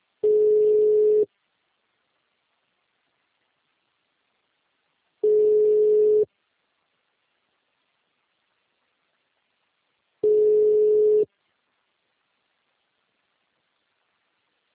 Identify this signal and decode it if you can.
Ringback tone (Europe, 425 Hz).